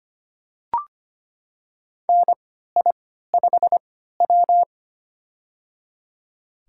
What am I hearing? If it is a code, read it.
Morse code, heard as a keyed tone, about 25 wpm, spaced as for 14 wpm, 700 Hz. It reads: NI5W
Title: USA Call Signs - 3x Character Spacing - 25wpm